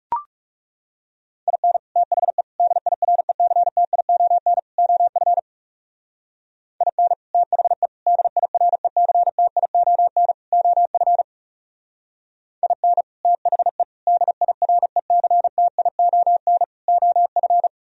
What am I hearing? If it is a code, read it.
Morse code, heard as a keyed tone, about 36 wpm, 700 Hz. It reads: IN THE DIRECTION OF IN THE DIRECTION OF IN THE DIRECTION OF
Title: Speed-Racing - Sets of 4 Words 30wpm